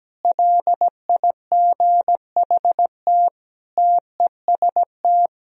Morse code, heard as a keyed tone, about 17 wpm, 700 Hz. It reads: LIGHT TEST
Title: Sentences from Top 1000 Words - Review Part 3 - 17wpm